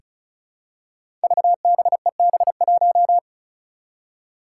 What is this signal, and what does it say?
Morse code, heard as a keyed tone, about 36 wpm, 700 Hz. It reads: VBEB1
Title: Five Challenging Letter-Number Combinations 35wpm